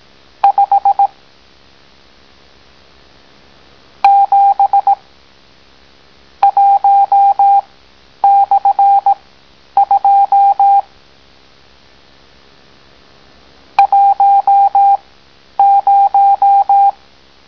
Morse code, heard as a keyed tone, about 18 wpm, spaced as for 10 wpm, 800 Hz. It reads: 5 7 1/2 10